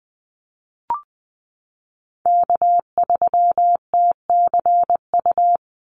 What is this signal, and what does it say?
Morse code, heard as a keyed tone, about 20 wpm, 700 Hz. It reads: K3TCU
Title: NA SOTA Chaser Callsigns 20wpm